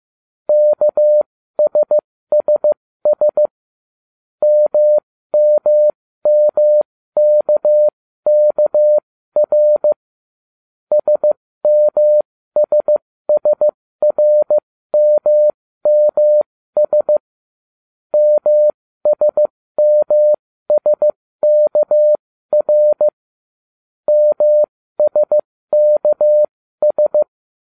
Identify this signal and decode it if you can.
Morse code, heard as a keyed tone, about 15 wpm, spaced as for 13 wpm, 600 Hz. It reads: KSSS MMMKKR SMSSRMMS MSMSKR MSKS